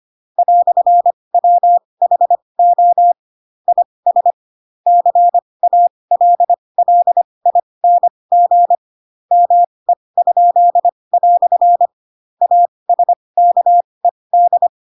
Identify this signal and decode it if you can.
Morse code, heard as a keyed tone, about 25 wpm, spaced as for 20 wpm, 700 Hz. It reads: "WHO IS CALLING ME?" ASKED
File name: Pinocchio / Pinocchio ch4 15 25 ik1hge